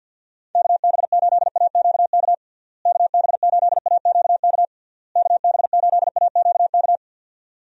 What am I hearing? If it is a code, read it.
Morse code, heard as a keyed tone, about 51 wpm, 700 Hz. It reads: KB8AQK KB8AQK KB8AQK
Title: USA Call Signs - 6 Characters - Repeated 3x - 50wpm